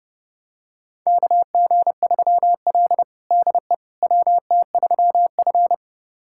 Morse code, heard as a keyed tone, about 30 wpm, 700 Hz. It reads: KG3L DE WT3F
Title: USA Callsign DE Callsign - 4 Characters - 30wpm